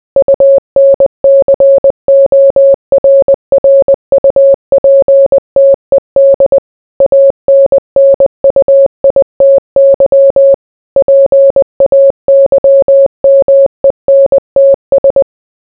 Morse code, heard as a keyed tone, about 20 wpm, 550 Hz. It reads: UNCORRUPTED ANNUITY PAYMENTS